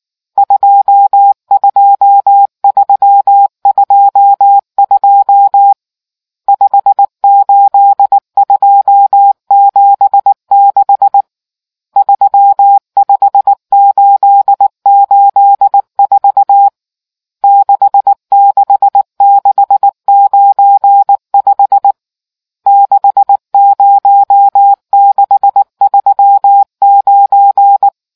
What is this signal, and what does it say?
Morse code, heard as a keyed tone, about 19 wpm, 800 Hz. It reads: 22322 58276 35884 66695 60639